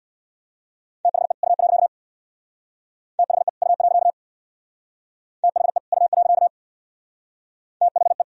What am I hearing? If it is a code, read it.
Morse code, heard as a keyed tone, about 62 wpm, 700 Hz. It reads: THE U. THE U. THE U. THE
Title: Speed-Racing - Sets of 2 Words 50wpm